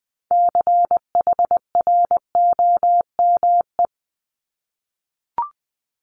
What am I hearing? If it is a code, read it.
Morse code, heard as a keyed tone, about 20 wpm, 700 Hz. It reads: CHROME
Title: Multi-letter Phonemes Repeated 3x 20wpm